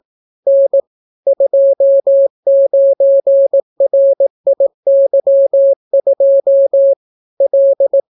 Morse code, heard as a keyed tone, about 18 wpm, 550 Hz. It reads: N 29RIY2 L